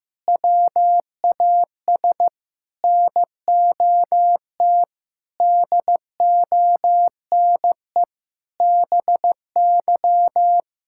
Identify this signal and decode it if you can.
Morse code, heard as a keyed tone, about 15 wpm, 700 Hz. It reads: WAS NOT DONE BY
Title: Sentences from Top 1000 Words - Review Part 5 - 15wpm